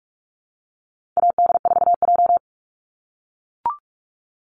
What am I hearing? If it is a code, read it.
Morse code, heard as a keyed tone, about 44 wpm, 700 Hz. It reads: AD4J
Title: Call Signs - 45wpm